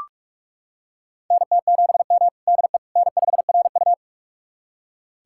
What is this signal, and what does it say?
Morse code, heard as a keyed tone, about 44 wpm, 690 Hz. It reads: NT7M DE N5RU
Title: USA Callsign DE Callsign - 4 Characters - 45wpm